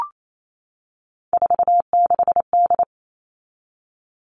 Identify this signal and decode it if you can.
Morse code, heard as a keyed tone, about 28 wpm, 700 Hz. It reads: 46D
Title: All Cut Numbers - 3-Digit 28wpm